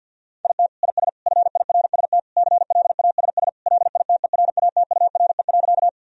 Morse code, heard as a keyed tone, about 50 wpm, 700 Hz. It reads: IT IS FIRST CLASS LITERATURE.
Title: Sentences from Top 1000 Words - Review Part 3 - 50wpm